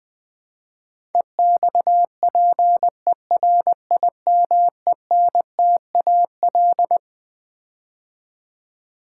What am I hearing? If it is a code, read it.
Morse code, heard as a keyed tone, about 20 wpm, 700 Hz. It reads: EXPERIMENTAL
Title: Build A Word - 5 Letters At A Time - 12-Letter Words - 20wpm